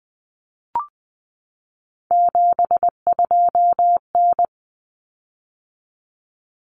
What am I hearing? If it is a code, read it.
Morse code, heard as a keyed tone, about 20 wpm, 700 Hz. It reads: 72N
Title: Common Cut Numbers - 3-Digit 20wpm